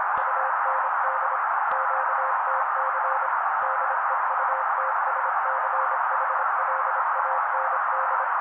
Morse code, heard as a keyed tone, about 26 wpm, 530 Hz. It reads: UND QTC DEUTSCHLAND
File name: daily / 25wpm noise